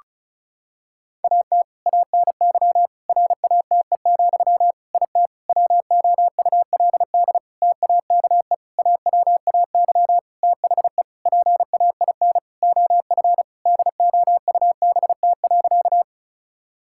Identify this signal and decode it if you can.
Morse code, heard as a keyed tone, about 34 wpm, 700 Hz. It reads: AT ANY RATE, IT WOULD TAKE AWAY THE PAIN OF DOUBT.
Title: Sentences from Top 1600 Words 35wpm